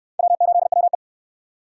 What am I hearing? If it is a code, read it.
Morse code, heard as a keyed tone, about 68 wpm, 700 Hz. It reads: W8WE